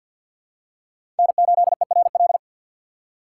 Morse code, heard as a keyed tone, about 51 wpm, 700 Hz. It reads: N8ERL
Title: USA Call Signs - 5 Characters - 6x Character Spacing - 50wpm